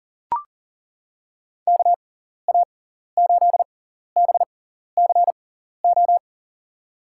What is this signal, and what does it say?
Morse code, heard as a keyed tone, about 38 wpm, spaced as for 14 wpm, 700 Hz. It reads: KA8BCO
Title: USA Call Signs - 6 Characters - 6x Character Spacing - 40wpm